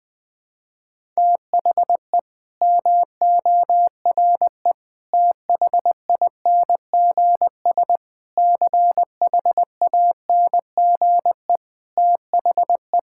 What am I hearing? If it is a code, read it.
Morse code, heard as a keyed tone, about 21 wpm, 700 Hz. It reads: THE MORE THINGS CHANGE THE
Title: Sets of 8 Words - 20wpm